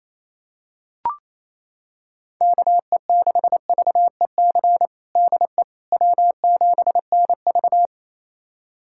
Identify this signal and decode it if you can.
Morse code, heard as a keyed tone, about 28 wpm, 700 Hz. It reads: KE6VEC DE W7NV
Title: USA Callsign DE Callsign 28wpm